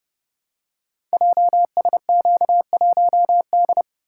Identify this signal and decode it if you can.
Morse code, heard as a keyed tone, about 30 wpm, 700 Hz. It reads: JSQ1D
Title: Code Groups - 5 Letter:Numbers - 30wpm